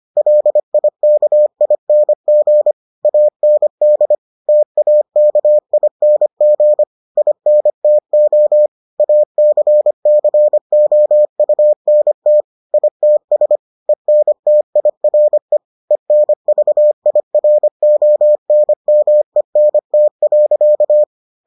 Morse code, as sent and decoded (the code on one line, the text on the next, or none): .-.. .. -.- .. -. --. / .- -. -.. / - .- -.- .. -. --. / .. -. - --- / .- -.-. -.-. --- ..- -. - / .. - ... / . -. - .. .-. / . -. ...- .. .-. --- -. -- . -. - .-.-.-
LIKING AND TAKING INTO ACCOUNT ITS ENTIRE ENVIRONMENT.